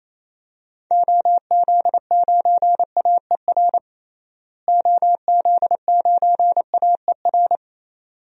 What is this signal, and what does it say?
Morse code, heard as a keyed tone, about 28 wpm, 700 Hz. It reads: OZ9AER OZ9AER
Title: DXCC-100 Call Signs - 6 Characters - Repeated 2x - 28wpm